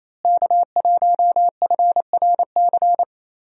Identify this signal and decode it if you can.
Morse code, heard as a keyed tone, about 28 wpm, 700 Hz. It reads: K1FRC
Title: USA Call Signs - 5 Characters - Repeated 2x - 28wpm